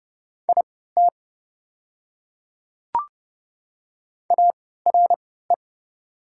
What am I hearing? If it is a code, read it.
Morse code, heard as a keyed tone, about 29 wpm, spaced as for 17 wpm, 700 Hz. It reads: IT ARE
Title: 3x Character Spacing - Top 200 Words - 30wpm